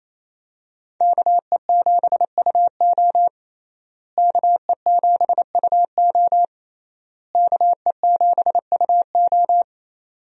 Morse code, heard as a keyed tone, about 27 wpm, 700 Hz. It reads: KE7UO KE7UO KE7UO
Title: USA Call Signs - 5 Characters - Repeated 3x - 28wpm